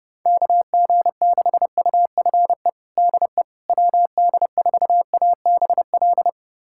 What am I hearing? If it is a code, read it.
Morse code, heard as a keyed tone, about 32 wpm, 700 Hz. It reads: KG6UFE DE WD4ABL
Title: USA Callsign DE Callsign 30wpm